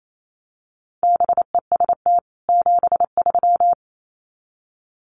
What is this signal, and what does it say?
Morse code, heard as a keyed tone, about 28 wpm, 700 Hz. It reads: BEST 73
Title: Common QSO Phrases 28wpm